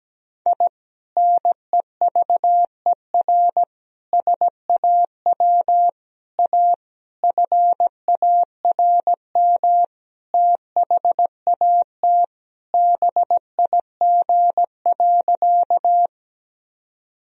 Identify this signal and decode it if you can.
Morse code, heard as a keyed tone, about 17 wpm, 700 Hz. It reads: I NEVER SAW A FARM THAT BIG.
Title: Sentences from Top 1000 Words - Review Part 2 - 17wpm